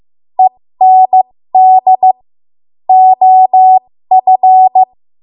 Morse code, heard as a keyed tone, about 15 wpm, spaced as for 13 wpm, 750 Hz. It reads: END OF
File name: Morse / 13WPM